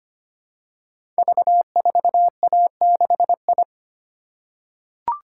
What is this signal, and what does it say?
Morse code, heard as a keyed tone, about 25 wpm, 700 Hz. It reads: V4A6I
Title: Code Groups - 5 Letter:Numbers - 25wpm